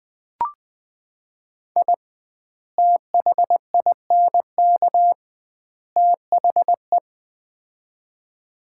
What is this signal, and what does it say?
Morse code, heard as a keyed tone, about 20 wpm, 700 Hz. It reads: I THINK THE
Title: 2x Word-Spacing - Sets of 3 Words - 20wpm